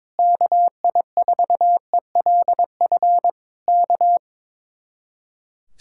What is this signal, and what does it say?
Morse code, heard as a keyed tone, about 21 wpm, 700 Hz. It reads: KI4ELF K
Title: Speed-Racing Common QSO Phrases - 22wpm